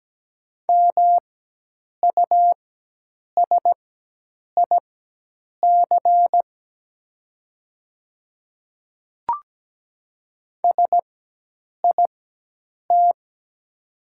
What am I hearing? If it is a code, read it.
Morse code, heard as a keyed tone, about 17 wpm, spaced as for 8 wpm, 700 Hz. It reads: MUSIC SIT
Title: 4x Character Spacing - Top 500 Words - 17wpm